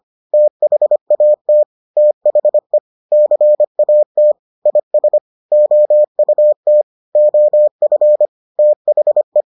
Morse code, heard as a keyed tone, about 25 wpm, 600 Hz. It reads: THAT THE CAT IS OUT OF THE